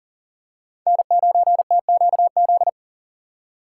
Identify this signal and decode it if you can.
Morse code, heard as a keyed tone, about 40 wpm, 700 Hz. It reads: N9TQZ
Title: USA Call Signs - 5 Characters - 3x Character Spacing - 40wpm